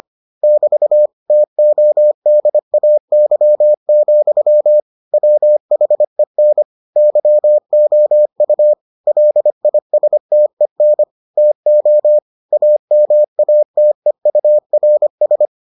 Morse code, heard as a keyed tone, about 25 wpm, 600 Hz. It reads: <BT> TODAY, WHEN YOU LISTEN TO AMATEURS